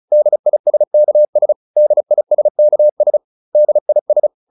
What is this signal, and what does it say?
Morse code, heard as a keyed tone, about 34 wpm, 600 Hz. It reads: DISKS DISKS DIS